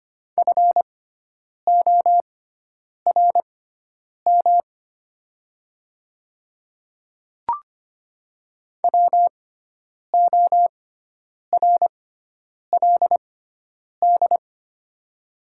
Morse code, heard as a keyed tone, about 25 wpm, spaced as for 9 wpm, 700 Hz. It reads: FORM WORLD
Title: 6x Character Spacing - Top 300 Words - 25wpm